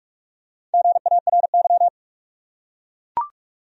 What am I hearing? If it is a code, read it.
Morse code, heard as a keyed tone, about 43 wpm, 700 Hz. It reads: GARY
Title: USA Common Names - 45wpm